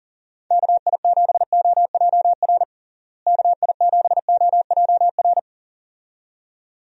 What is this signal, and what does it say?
Morse code, heard as a keyed tone, about 40 wpm, 700 Hz. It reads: KI7OJR KI7OJR
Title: USA Call Signs - 6 Characters - Repeated 2x - 40wpm